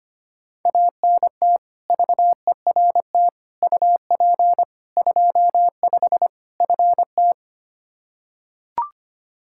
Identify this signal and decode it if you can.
Morse code, heard as a keyed tone, about 25 wpm, 700 Hz. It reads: ANT VERT UP 25 FT
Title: K1USN SST QSO Practice 25wpm